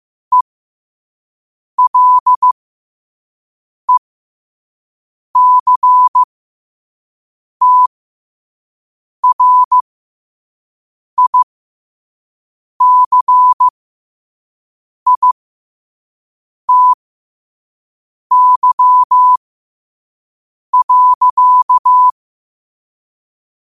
Morse code, heard as a keyed tone, about 15 wpm, spaced as for 5 wpm, 1000 Hz. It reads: ELECTRICITY.
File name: 5wpm